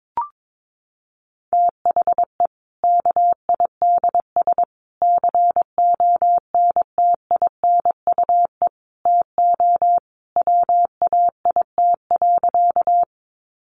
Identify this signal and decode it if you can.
Morse code, heard as a keyed tone, about 22 wpm, 700 Hz. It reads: THE KIDS CONTINUE TO WAIT.
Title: Easy Phrases 22wpm